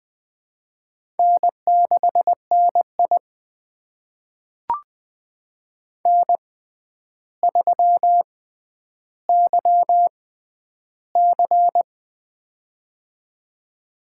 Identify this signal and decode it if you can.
Morse code, heard as a keyed tone, about 20 wpm, spaced as for 10 wpm, 700 Hz. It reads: N6NI N3YC
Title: USA Call Signs - 4 Characters - 6x Character Spacing - 20wpm